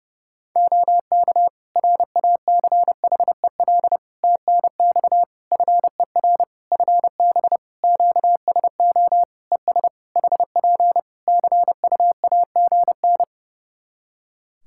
Morse code, heard as a keyed tone, about 30 wpm, 700 Hz. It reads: OK RACHEL TNX FER FB QSO ES HP CUAGN
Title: K1USN SST QSO Practice 30wpm